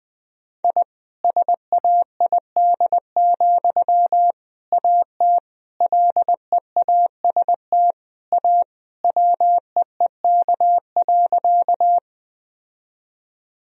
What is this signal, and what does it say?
Morse code, heard as a keyed tone, about 21 wpm, 700 Hz. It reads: I SAID, AT LEAST A WEEK.
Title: Sentences from Top 1000 Words - Review Part 2 - 20wpm